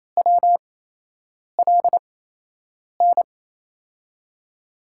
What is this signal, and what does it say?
Morse code, heard as a keyed tone, about 28 wpm, spaced as for 8 wpm, 700 Hz. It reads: WLN
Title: USA Call Signs - 6 Characters - 8x Character Spacing - 28wpm